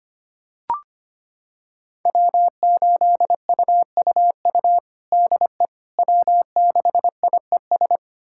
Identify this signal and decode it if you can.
Morse code, heard as a keyed tone, about 25 wpm, 700 Hz. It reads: W8UUU DE W6IES
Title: USA Callsign DE Callsign - 5 Characters - 25wpm